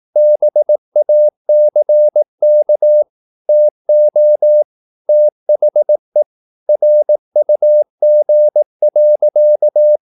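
Morse code, heard as a keyed tone, about 18 wpm, 600 Hz. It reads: BACK TO THE RUG.
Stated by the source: Morse practice at 18 wpm, English text, from CW med Gnister.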